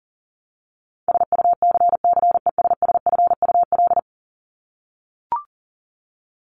Morse code, heard as a keyed tone, about 40 wpm, 700 Hz. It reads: SUCCESSFUL
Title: Top 751-1000 Words 40wpm